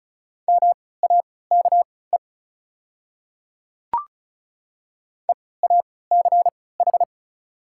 Morse code, heard as a keyed tone, about 34 wpm, spaced as for 20 wpm, 690 Hz. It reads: MAKE EACH
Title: 3x Character Spacing - Top 100 Words - 35wpm